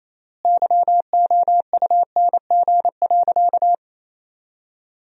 Morse code, heard as a keyed tone, about 28 wpm, 700 Hz. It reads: YOUNG.
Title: 3x Word-Spacing - Sentences from Top 400 Words - 28wpm